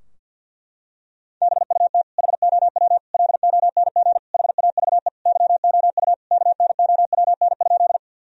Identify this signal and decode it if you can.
Morse code, heard as a keyed tone, about 49 wpm, 700 Hz. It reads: BUT HOW LONG HAVE YOU KNOWN?
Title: Sentences from Top 1000 Words - Review Part 3 - 50wpm